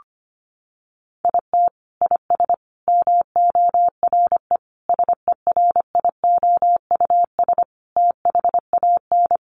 Morse code, heard as a keyed tone, about 25 wpm, 700 Hz. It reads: IT IS MORE SERIOUS THAN
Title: Sentences from Top 1100 Words 25wpm